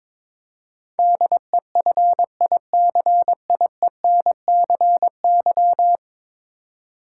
Morse code, heard as a keyed tone, about 22 wpm, 700 Hz. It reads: DEFICIENCY
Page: morse-code-ninja-practice-22wpm